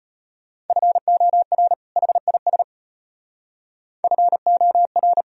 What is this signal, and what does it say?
Morse code, heard as a keyed tone, about 30 wpm, 700 Hz. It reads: FOR HIS FOR
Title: Speed-Racing - Sets of 2 Words 25wpm